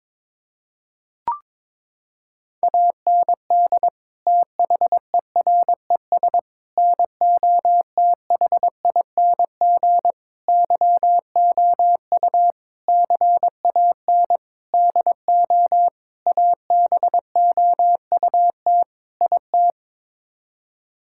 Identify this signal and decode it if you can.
Morse code, heard as a keyed tone, about 22 wpm, 700 Hz. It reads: AND THERES NOTHING YOU CAN DO ABOUT IT
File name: practice / 22wpm / Sets of 8 Words - 22wpm